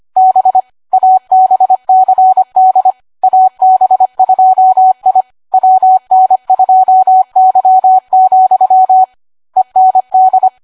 Morse code, heard as a keyed tone, about 25 wpm, 750 Hz. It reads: B ABCD AB2I WN2Y, END